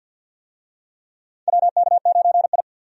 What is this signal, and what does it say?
Morse code, heard as a keyed tone, about 46 wpm, 700 Hz. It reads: WK9I
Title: USA Call Signs - 4 Characters - 50wpm